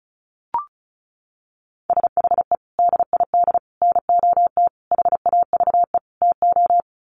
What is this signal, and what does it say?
Morse code, heard as a keyed tone, about 34 wpm, 700 Hz. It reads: SHE DID NOT HAVE TO
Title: Sets of 5 Words Encore2 35wpm